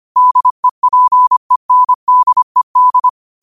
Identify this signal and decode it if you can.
Morse code, heard as a keyed tone, about 25 wpm, 1000 Hz. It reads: DEPENDED
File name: qotdfiles / 25wpm